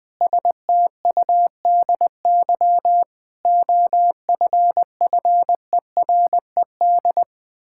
Morse code, heard as a keyed tone, about 19 wpm, 700 Hz. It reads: STUDY OFFERED
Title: Sentences with 8-Word Phrases - 20wpm